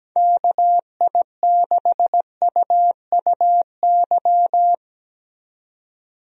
Morse code, heard as a keyed tone, about 17 wpm, 700 Hz. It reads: KI6UUY
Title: USA Call Signs - 6 Characters - 17wpm